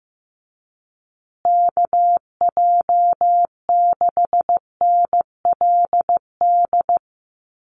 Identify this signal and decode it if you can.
Morse code, heard as a keyed tone, about 15 wpm, 700 Hz. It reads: KJ6NLD
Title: USA Call Signs Repeated 2x 15wpm